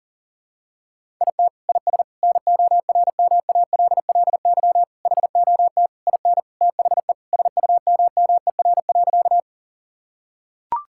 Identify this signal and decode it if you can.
Morse code, heard as a keyed tone, about 40 wpm, 700 Hz. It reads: IT IS NORMALLY HOT IN THE SUMMER.
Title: Sentences from Top 1000 Words - Review Part 6 - 40wpm